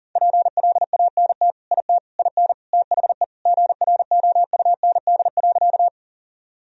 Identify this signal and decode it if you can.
Morse code, heard as a keyed tone, about 38 wpm, 700 Hz. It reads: PLANT IT IN THE GROUND.
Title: Sentences from Top 1000 Words - Review Part 1 - 40wpm